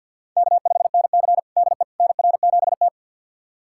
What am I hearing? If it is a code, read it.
Morse code, heard as a keyed tone, about 50 wpm, 700 Hz. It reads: K5NC DE NR7T
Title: USA Callsign DE Callsign - 4 Characters - 50wpm